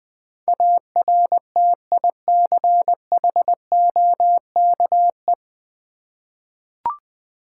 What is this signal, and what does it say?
Morse code, heard as a keyed tone, about 21 wpm, 690 Hz. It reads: ARTICHOKE
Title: Difficult Words with Hints - 20wpm